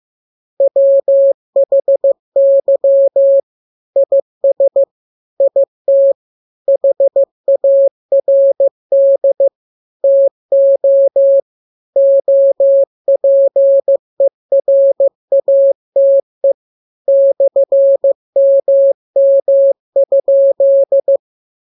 Morse code, heard as a keyed tone, about 15 wpm, 550 Hz. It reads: WHY IS IT HARD TO OPERATE /MM?